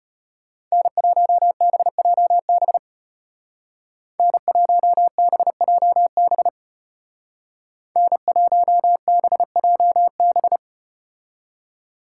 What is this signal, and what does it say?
Morse code, heard as a keyed tone, about 33 wpm, 700 Hz. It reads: N1BJB N1BJB N1BJB